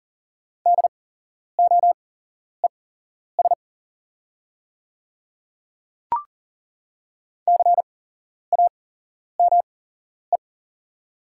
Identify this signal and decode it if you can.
Morse code, heard as a keyed tone, about 40 wpm, spaced as for 11 wpm, 690 Hz. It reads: DOES CAME
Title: 8x Character Spacing - Top 500 Words - 40wpm